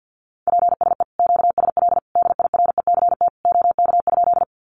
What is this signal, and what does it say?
Morse code, heard as a keyed tone, about 50 wpm, 700 Hz. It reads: PSE QSL DIRECT OK?
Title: Common QSO Phrases 50wpm